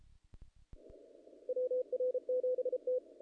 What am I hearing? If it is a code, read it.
Morse code, heard as a keyed tone, about 32 wpm, 470 Hz. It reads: WR7T
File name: contests / naqp-cw-k5tr-op-k5pi- -1800